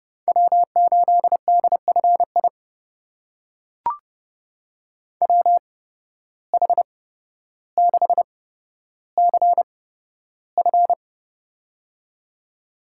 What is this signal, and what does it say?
Morse code, heard as a keyed tone, about 32 wpm, spaced as for 13 wpm, 700 Hz. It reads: W8DFI WH6CF
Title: USA Call Signs - 8x Character Spacing - 30wpm